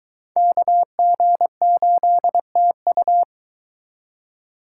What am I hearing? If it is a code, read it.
Morse code, heard as a keyed tone, about 22 wpm, 700 Hz. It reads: KG8TU